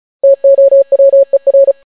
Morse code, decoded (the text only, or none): TOWER